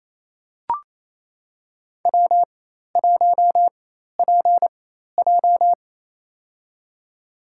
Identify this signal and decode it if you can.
Morse code, heard as a keyed tone, about 28 wpm, spaced as for 13 wpm, 700 Hz. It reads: W1PJ
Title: USA Call Signs - 4 Characters - 4x Character Spacing - 28wpm